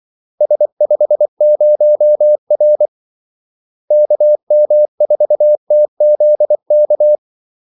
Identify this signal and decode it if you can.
Morse code, heard as a keyed tone, about 24 wpm, 600 Hz. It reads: S50R KM4TZK
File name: Callsigns-24wpm